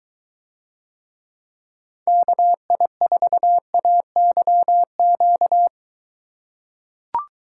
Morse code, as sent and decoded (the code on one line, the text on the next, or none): -.- .. ....- .- -.-- --.-
KI4AYQ